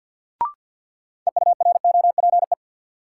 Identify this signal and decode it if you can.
Morse code, heard as a keyed tone, about 53 wpm, 700 Hz. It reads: EUROPE
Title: Sets of 5 Words Encore - New Word Drill - 50wpm